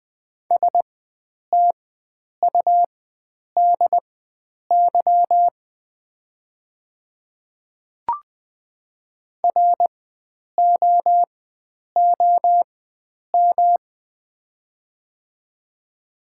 Morse code, heard as a keyed tone, about 20 wpm, spaced as for 9 wpm, 700 Hz. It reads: STUDY ROOM
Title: 4x Character Spacing - Top 300 Words - 20wpm